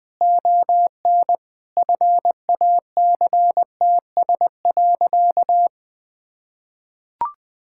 Morse code, heard as a keyed tone, about 21 wpm, 700 Hz. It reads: ON FACTS.
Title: Sentences with 2-Word Phrases - 20wpm